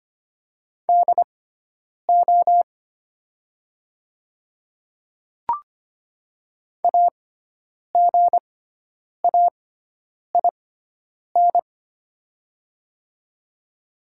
Morse code, heard as a keyed tone, about 25 wpm, spaced as for 9 wpm, 690 Hz. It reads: DO AGAIN
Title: 6x Character Spacing - Top 200 Words - 25wpm